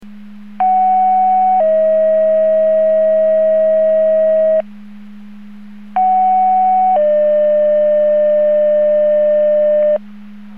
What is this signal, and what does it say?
Two-tone page, heard as a one-second tone, then a three-second tone, about 750 then 640 Hz, about 750 then 600 Hz. It is heard from 0.6 to 10.0 s.
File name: Portage 154.385 Paging